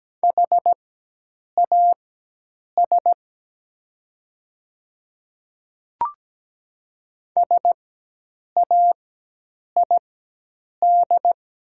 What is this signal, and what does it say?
Morse code, heard as a keyed tone, about 17 wpm, spaced as for 8 wpm, 700 Hz. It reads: HAS SAID
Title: 4x Character Spacing - Top 100 Words - 17wpm